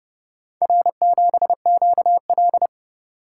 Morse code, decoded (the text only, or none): R7QL